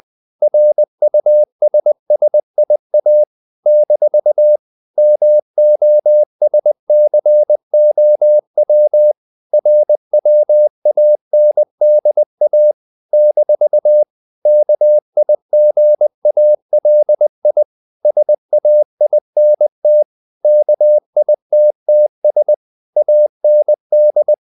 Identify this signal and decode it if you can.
Morse code, heard as a keyed tone, about 20 wpm, 600 Hz. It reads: RUSSIA - MOSCOW RWANDA - KIGALI SAINT KITTS AND
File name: CW / Practice / NationalCapitals20